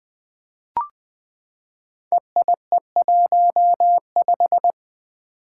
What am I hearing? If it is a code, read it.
Morse code, heard as a keyed tone, about 20 wpm, 700 Hz. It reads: EIE15
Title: Five Challenging Letter-Number Combinations 20wpm